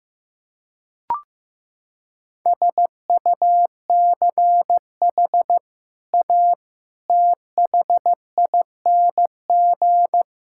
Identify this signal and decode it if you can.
Morse code, heard as a keyed tone, about 15 wpm, 700 Hz. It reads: SUCH A THING
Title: Sentences from Top 1000 Words - Review Part 6 - 15wpm